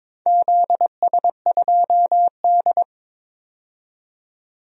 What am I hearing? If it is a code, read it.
Morse code, heard as a keyed tone, about 22 wpm, 700 Hz. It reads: ZS2D
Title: DXCC-100 Call Signs - 4 Characters - 22wpm